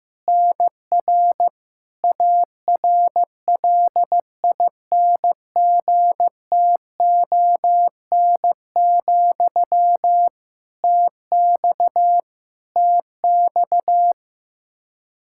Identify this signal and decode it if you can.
Morse code, heard as a keyed tone, about 15 wpm, 700 Hz. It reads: NR ARLINGTON, TX TX
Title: Practice QSO - Exchange 1 - Beginning - 15wpm